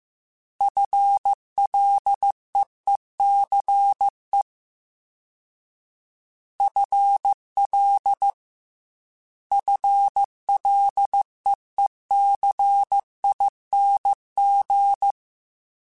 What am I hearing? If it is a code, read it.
Morse code, heard as a keyed tone, about 15 wpm, 800 Hz. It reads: FLEECE FL FLEECING
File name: FL / FL mp3s / FL Prefixes 1x1 @13